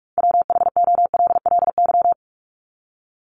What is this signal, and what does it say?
Morse code, heard as a keyed tone, about 44 wpm, 700 Hz. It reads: WHOLLY
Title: LY Suffix - Top 100 Words with Base Words 45wpm